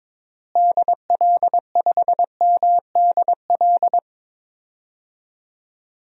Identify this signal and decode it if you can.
Morse code, heard as a keyed tone, about 22 wpm, 700 Hz. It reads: DL5MDL
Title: DXCC-100 Call Signs - 6 Characters - Repeated 2x - 22wpm